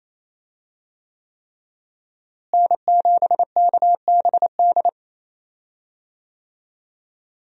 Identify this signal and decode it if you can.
Morse code, heard as a keyed tone, about 28 wpm, 700 Hz. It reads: N7KBD
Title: USA Call Signs - 5 Characters - Repeated 3x - 28wpm